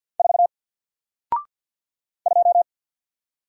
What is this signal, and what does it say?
Morse code, heard as a keyed tone, about 54 wpm, 700 Hz. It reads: 4 2
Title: Single Letter-Number Alt 50wpm